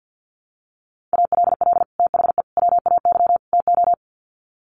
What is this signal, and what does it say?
Morse code, heard as a keyed tone, about 50 wpm, 700 Hz. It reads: ALL THE WAY TO
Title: Sets of 4 Words 50wpm